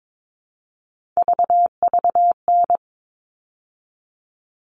Morse code, heard as a keyed tone, about 22 wpm, 700 Hz. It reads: VVN
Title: All Cut Numbers - 3-Digit 22wpm